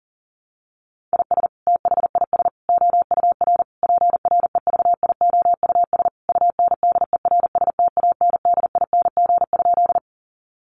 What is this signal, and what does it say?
Morse code, heard as a keyed tone, about 40 wpm, 700 Hz. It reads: IS THIS OUR PREVIOUS UNDERSTANDING?
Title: Sentences from Top 1500 Words 40wpm